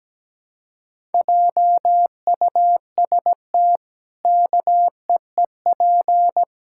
Morse code, heard as a keyed tone, about 17 wpm, 700 Hz. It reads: JUST KEEP
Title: Pangram Challenge v2 - 17wpm